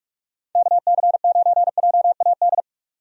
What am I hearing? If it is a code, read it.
Morse code, heard as a keyed tone, about 46 wpm, 700 Hz. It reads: KC9JAD
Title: USA Call Signs - 6 Characters - 45wpm